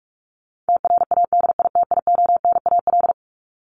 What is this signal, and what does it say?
Morse code, heard as a keyed tone, about 44 wpm, 700 Hz. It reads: TRADITIONAL
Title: Top 751-1000 Words 45wpm